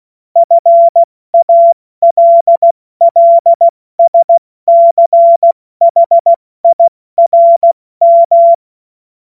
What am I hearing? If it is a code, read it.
Morse code, heard as a keyed tone, about 16 wpm, 680 Hz. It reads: FALLSCHIRM
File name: Morsekurs / Witze / Witz 31.Oktober 2023 70 F80